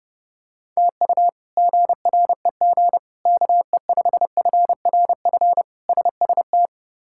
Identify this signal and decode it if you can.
Morse code, heard as a keyed tone, about 30 wpm, 700 Hz. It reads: TU GREG KE5FRF SST
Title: K1USN SST QSO Practice - 30wpm